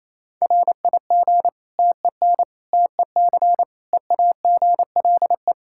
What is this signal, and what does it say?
Morse code, heard as a keyed tone, about 28 wpm, 700 Hz. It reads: RIG TEN TEC EAGLE
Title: Practice QSO - Exchange 2 - Info - 28wpm